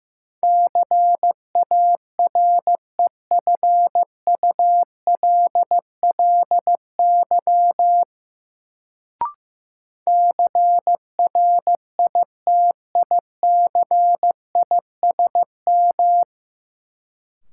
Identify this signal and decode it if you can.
Morse code, heard as a keyed tone, about 15 wpm, 700 Hz. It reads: CAREFULLY CRITICISM
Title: Sentences from Top 1300 Words - New Word Drill Alt - 15wpm